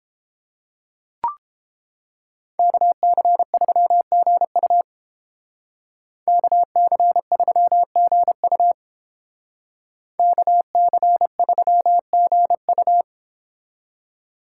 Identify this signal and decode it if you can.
Morse code, heard as a keyed tone, about 29 wpm, 700 Hz. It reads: KC3GU KC3GU KC3GU